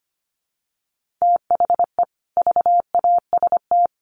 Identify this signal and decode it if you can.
Morse code, heard as a keyed tone, about 25 wpm, 700 Hz. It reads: THE VAST
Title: Sets of 4 Words 25wpm